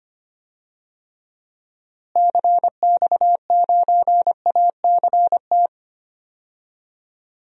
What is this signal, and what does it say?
Morse code, heard as a keyed tone, about 25 wpm, 700 Hz. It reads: CX9ACT
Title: DXCC-100 Call Signs - 6 Characters - Repeated 2x - 25wpm